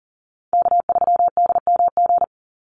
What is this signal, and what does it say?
Morse code, heard as a keyed tone, about 40 wpm, 700 Hz. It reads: K3DMG
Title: Call Signs - 40wpm